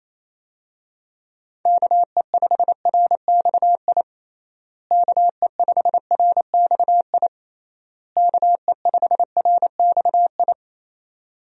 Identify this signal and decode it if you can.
Morse code, heard as a keyed tone, about 28 wpm, 700 Hz. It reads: KE5RXI KE5RXI KE5RXI
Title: USA Call Signs - 6 Characters - Repeated 3x - 28wpm